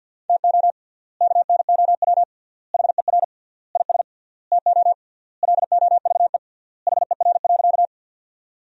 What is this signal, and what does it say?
Morse code, heard as a keyed tone, about 50 wpm, 700 Hz. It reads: TO KNOW HER IS TO LOVE HER.
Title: 3x Word-Spacing - Sentences from Top 400 Words - 50wpm